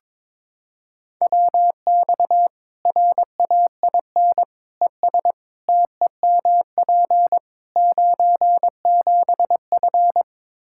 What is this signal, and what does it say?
Morse code, heard as a keyed tone, about 22 wpm, 700 Hz. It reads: WX RAIN ES TEMP 97F
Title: Practice QSO - Exchange 2 - Info - 22wpm